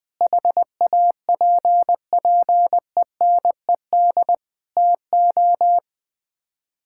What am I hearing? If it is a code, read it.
Morse code, heard as a keyed tone, about 21 wpm, 700 Hz. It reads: HAPPENED TO
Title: Sets of 8 Words - 20wpm